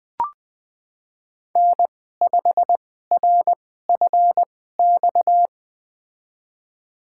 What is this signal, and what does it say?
Morse code, heard as a keyed tone, about 20 wpm, spaced as for 15 wpm, 700 Hz. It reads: N5RFX
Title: USA Call Signs - 5 Characters - 2x Character Spacing - 20wpm